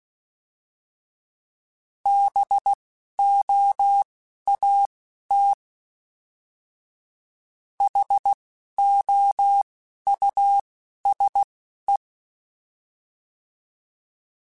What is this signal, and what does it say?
Morse code, heard as a keyed tone, about 16 wpm, spaced as for 11 wpm, 800 Hz. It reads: BOAT HOUSE
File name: General / General-mp3s / Houses @10